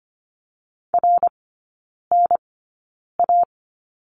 Morse code, heard as a keyed tone, about 25 wpm, spaced as for 9 wpm, 690 Hz. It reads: RNA
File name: Lesson 14e - Number 2 - Three Characters 25wpm